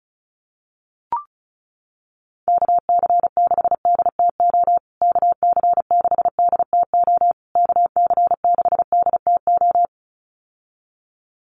Morse code, heard as a keyed tone, about 34 wpm, 700 Hz. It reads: KC6DTO KC6DTO KC6DTO
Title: USA Call Signs Repeated 3x 35wpm